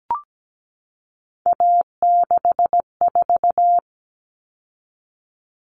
Morse code, heard as a keyed tone, about 17 wpm, 700 Hz. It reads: A64
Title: All Cut Numbers - 3-Digit 17wpm